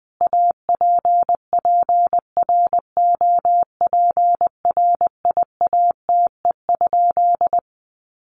Morse code, heard as a keyed tone, about 20 wpm, 700 Hz. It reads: APPROPRIATE?
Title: Sentences from Top 900 Words 20wpm